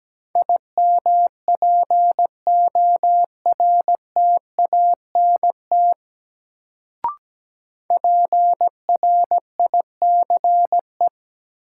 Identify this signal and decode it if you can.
Morse code, heard as a keyed tone, about 17 wpm, 700 Hz. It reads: IMPORTANT PRICE
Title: Sentences from Top 600 Words - New Word Drill - 17wpm